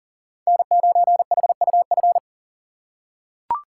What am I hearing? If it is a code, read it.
Morse code, heard as a keyed tone, about 40 wpm, 700 Hz. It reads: N9HUF
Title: USA Call Signs - 2x Character Spacing - 40wpm